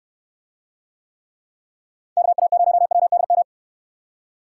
Morse code, heard as a keyed tone, about 69 wpm, 700 Hz. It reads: KA0WDM